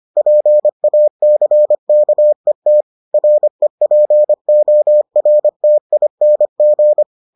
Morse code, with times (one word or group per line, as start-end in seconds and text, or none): 0.2-2.8 PACKET
3.1-7.0 REPORTING